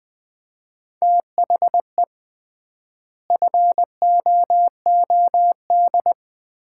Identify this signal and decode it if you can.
Morse code, heard as a keyed tone, about 21 wpm, 700 Hz. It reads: THE FOOD
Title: 3x Word-Spacing - Sets of 5 Words - 20wpm